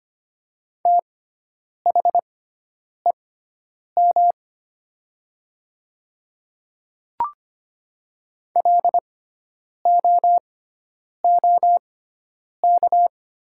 Morse code, heard as a keyed tone, about 25 wpm, spaced as for 9 wpm, 700 Hz. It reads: THEM LOOK
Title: 6x Character Spacing - Top 200 Words - 25wpm